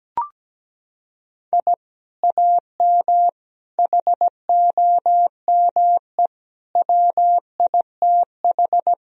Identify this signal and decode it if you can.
Morse code, heard as a keyed tone, about 17 wpm, 700 Hz. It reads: I AM HOME WITH
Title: Sentences from Top 1000 Words - Review Part 3 - 17wpm